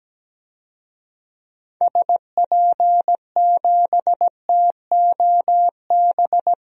Morse code, heard as a keyed tone, about 17 wpm, 700 Hz. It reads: SP7TOB
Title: DXCC-100 Call Signs - 6 Characters - Repeated 3x - 17wpm